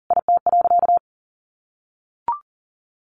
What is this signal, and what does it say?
Morse code, heard as a keyed tone, about 40 wpm, 700 Hz. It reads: IT.
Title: Sentences from Top 1300 Words 40wpm